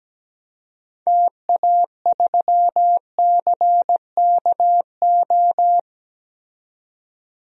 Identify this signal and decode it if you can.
Morse code, heard as a keyed tone, about 17 wpm, 700 Hz. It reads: TA3CKO
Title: DXCC-100 Call Signs - 6 Characters - 17wpm